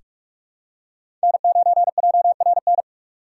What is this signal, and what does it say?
Morse code, heard as a keyed tone, about 45 wpm, 700 Hz. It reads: N9JRN
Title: USA Call Signs - 4x Character Spacing - 45wpm